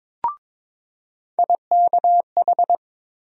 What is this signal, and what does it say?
Morse code, heard as a keyed tone, about 22 wpm, 700 Hz. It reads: IKH
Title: USA Call Sign Postfixes - 3 Letters 22wpm